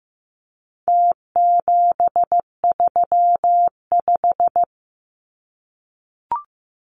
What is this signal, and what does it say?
Morse code, heard as a keyed tone, about 15 wpm, 700 Hz. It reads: T735
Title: Numbers - Contest Style - 4-Digit 15wpm